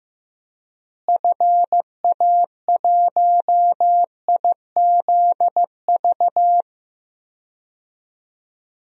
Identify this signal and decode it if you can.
Morse code, heard as a keyed tone, about 15 wpm, 700 Hz. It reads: FA1IZV
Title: DXCC-100 Call Signs - 6 Characters - Repeated 3x - 15wpm